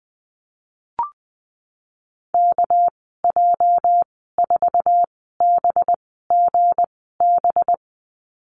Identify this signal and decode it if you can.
Morse code, heard as a keyed tone, about 19 wpm, spaced as for 14 wpm, 700 Hz. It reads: KJ4BGB
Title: 219 Lesson 35g - Number 0 - Call Signs 20wpm